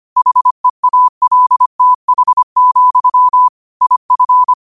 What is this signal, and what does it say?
Morse code, heard as a keyed tone, about 25 wpm, 1000 Hz. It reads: HEALTH, IF
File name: qotdfiles / 25wpm